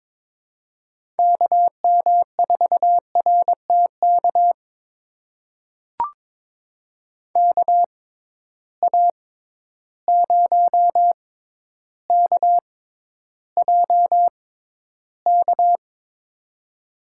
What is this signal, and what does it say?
Morse code, heard as a keyed tone, about 22 wpm, spaced as for 11 wpm, 700 Hz. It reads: KM4RTK KA0KJK
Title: USA Call Signs - 6 Characters - 6x Character Spacing - 22wpm